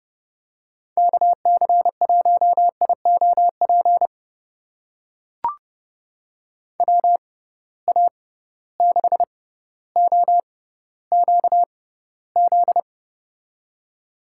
Morse code, heard as a keyed tone, about 30 wpm, spaced as for 15 wpm, 700 Hz. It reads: KC1IOP WA6OQZ
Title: USA Call Signs - 6 Characters - 6x Character Spacing - 30wpm